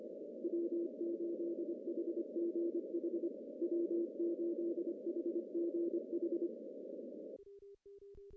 Morse code, heard as a keyed tone, about 25 wpm, 340 Hz, with noise. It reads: W8HGH W8HGH
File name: CW / cqww-cw-k5tr